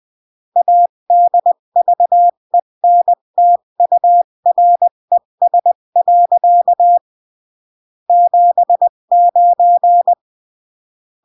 Morse code, heard as a keyed tone, about 20 wpm, 700 Hz. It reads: ADVENTURES. 79
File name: CW / jodorowsky / 82 commandments 20-15